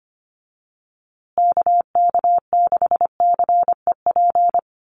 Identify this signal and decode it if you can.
Morse code, heard as a keyed tone, about 25 wpm, 700 Hz. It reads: KK6CEP
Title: USA Call Signs Repeated 3x 25wpm